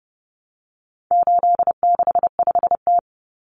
Morse code, heard as a keyed tone, about 30 wpm, 700 Hz. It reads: 865T
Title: Common Cut Numbers - 4-Digit 30wpm